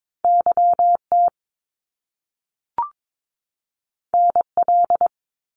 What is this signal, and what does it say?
Morse code, heard as a keyed tone, about 22 wpm, 700 Hz. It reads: YT NL
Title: Canadian Provinces and Territories 22wpm